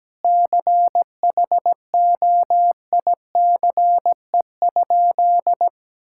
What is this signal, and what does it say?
Morse code, heard as a keyed tone, about 17 wpm, 700 Hz. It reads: CHOICE?
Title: Sentences from Top 1000 Words - Review Part 6 - 17wpm